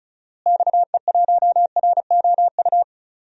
Morse code, heard as a keyed tone, about 34 wpm, 700 Hz. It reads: XE1ROU
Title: DXCC-100 Call Signs - 6 Characters - Repeated 3x - 35wpm